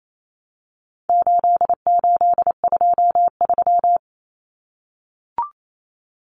Morse code, heard as a keyed tone, about 28 wpm, 700 Hz. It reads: 8823
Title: Numbers - 4-Digit 28wpm